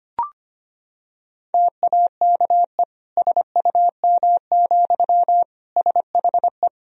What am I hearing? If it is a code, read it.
Morse code, heard as a keyed tone, about 25 wpm, 700 Hz. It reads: TAKE SUM, SHE
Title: Sentences from Top 200 Words and Abbreviations 25wpm